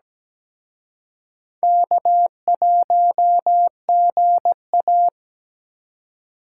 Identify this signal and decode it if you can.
Morse code, heard as a keyed tone, about 20 wpm, 700 Hz. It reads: K1GA
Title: Speed Racing - USA 1x2 Call Signs 17wpm